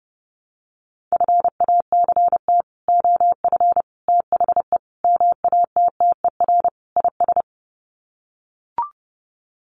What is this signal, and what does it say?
Morse code, heard as a keyed tone, about 30 wpm, 700 Hz. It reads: FACT OF THE MATTER IS
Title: Sets of 5 Words 30wpm